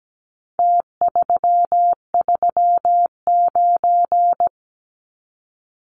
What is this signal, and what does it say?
Morse code, heard as a keyed tone, about 17 wpm, 700 Hz. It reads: T339
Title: Numbers - Contest Style - 4-Digit 17wpm